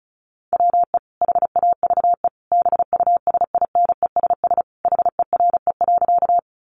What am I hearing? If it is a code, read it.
Morse code, heard as a keyed tone, about 34 wpm, 700 Hz. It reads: WE HAVE BUSINESS HERE.
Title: Sentences from Top 600 Words 35wpm